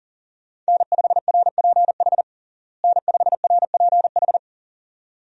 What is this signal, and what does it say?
Morse code, heard as a keyed tone, about 37 wpm, 700 Hz. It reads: N5RPH N5RPH
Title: USA Call Signs - 5 Characters - Repeated 2x - 40wpm